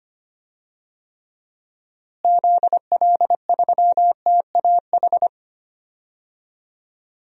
Morse code, heard as a keyed tone, about 25 wpm, 700 Hz. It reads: ZL3TAH
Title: DXCC-100 Call Signs - 6 Characters - Repeated 3x - 25wpm